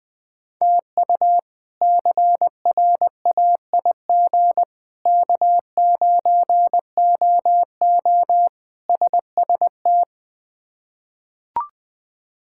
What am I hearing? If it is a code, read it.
Morse code, heard as a keyed tone, about 21 wpm, 700 Hz. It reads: TU CRAIG K9OO SST
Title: K1USN SST QSO Practice - 20wpm